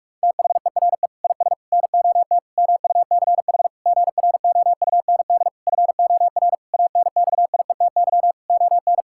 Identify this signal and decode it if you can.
Morse code, heard as a keyed tone, about 46 wpm, 700 Hz. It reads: THERE IS NOT MUCH GROUND FOR ANXIETY ON
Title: Sentences from Top 1600 Words 45wpm